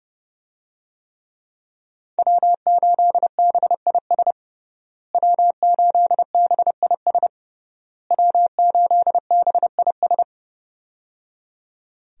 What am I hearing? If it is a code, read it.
Morse code, heard as a keyed tone, about 30 wpm, 700 Hz. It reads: W8BIS W8BIS W8BIS
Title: USA Call Signs - 5 Characters - Repeated 3x - 30wpm